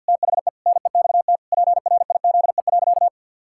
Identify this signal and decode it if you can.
Morse code, heard as a keyed tone, about 50 wpm, 700 Hz. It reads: THE NEXT PRIZE.
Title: Pangram Challenge v2 - 50wpm